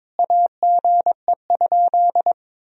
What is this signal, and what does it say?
Morse code, heard as a keyed tone, about 22 wpm, 700 Hz. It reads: AGE?
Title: Sentences from Top 1000 Words - Review Part 3 - 22wpm